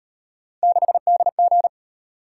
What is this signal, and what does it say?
Morse code, heard as a keyed tone, about 38 wpm, 700 Hz. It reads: 6DG